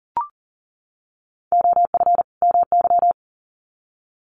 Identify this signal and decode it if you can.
Morse code, heard as a keyed tone, about 40 wpm, 700 Hz. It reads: OF MY
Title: Sets of 2 Words 40wpm